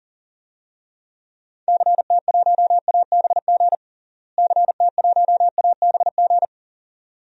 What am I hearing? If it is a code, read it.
Morse code, heard as a keyed tone, about 40 wpm, 700 Hz. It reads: CT1ABG CT1ABG
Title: DXCC-100 Call Signs - 6 Characters - Repeated 2x - 40wpm